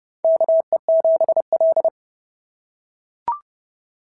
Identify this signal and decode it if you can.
Morse code, heard as a keyed tone, about 30 wpm, 690 Hz. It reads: KE7L
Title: SOTA QSO Practice 30wpm